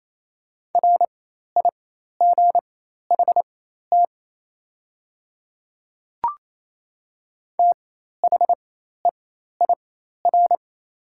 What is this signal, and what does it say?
Morse code, heard as a keyed tone, about 28 wpm, spaced as for 13 wpm, 700 Hz. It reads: RIGHT THEIR
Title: 4x Character Spacing - Top 200 Words - 28wpm